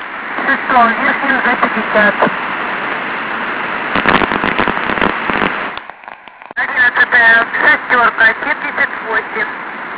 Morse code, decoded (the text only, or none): none